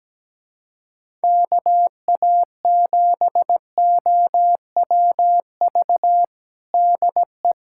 Morse code, heard as a keyed tone, about 17 wpm, 700 Hz. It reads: KA7OWV DE
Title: USA Callsign DE Callsign - 6 Characters - 17wpm